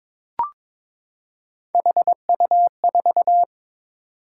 Morse code, heard as a keyed tone, about 22 wpm, 700 Hz. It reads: HU4
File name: practice / 22wpm / Three Letter-Number Combinations - 22wpm